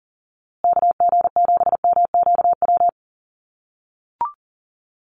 Morse code, heard as a keyed tone, about 40 wpm, 700 Hz. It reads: KG7MQW
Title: USA Call Signs Repeated 2x 40wpm